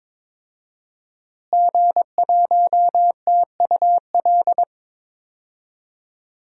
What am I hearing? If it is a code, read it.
Morse code, heard as a keyed tone, about 22 wpm, 700 Hz. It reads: G1TUL
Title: DXCC-100 Call Signs - 5 Characters - Repeated 3x - 22wpm